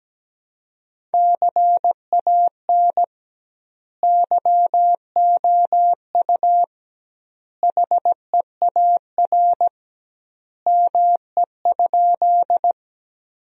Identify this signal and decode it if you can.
Morse code, heard as a keyed tone, about 17 wpm, 700 Hz. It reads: CAN YOU HEAR ME?
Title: 2x Word-Spacing - Sentences from Top 300 Words - 17wpm